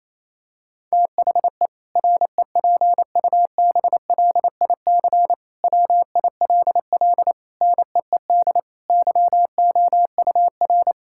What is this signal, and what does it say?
Morse code, heard as a keyed tone, about 27 wpm, 700 Hz. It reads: THE REPUBLIC WILL NEED YOUR
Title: Sentences from Top 1600 Words 28wpm